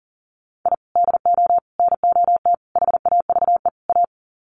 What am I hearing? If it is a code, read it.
Morse code, heard as a keyed tone, about 40 wpm, 700 Hz. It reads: I DO NOT HAVE A
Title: Sets of 5 Words 40wpm